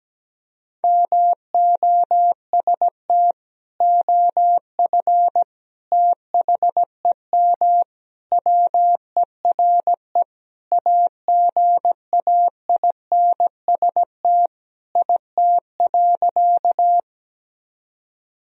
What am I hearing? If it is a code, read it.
Morse code, heard as a keyed tone, about 17 wpm, 700 Hz. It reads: MOST OF THEM WERE AGAINST IT.
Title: Sentences from Top 1000 Words - Review Part 1 - 17wpm